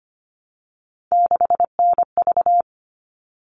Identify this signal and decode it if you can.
Morse code, heard as a keyed tone, about 25 wpm, 700 Hz. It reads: 6NV
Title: All Cut Numbers - 3-Digit 25wpm